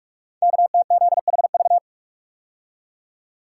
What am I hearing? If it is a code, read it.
Morse code, heard as a keyed tone, about 44 wpm, 700 Hz. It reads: KTZHV
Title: Code Groups - 5 Letters - 45wpm